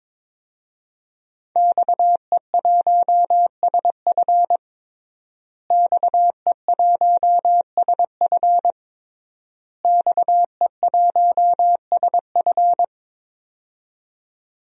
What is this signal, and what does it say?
Morse code, heard as a keyed tone, about 22 wpm, 700 Hz. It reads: XE1SF XE1SF XE1SF
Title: DXCC-100 Call Signs - 5 Characters - Repeated 3x - 22wpm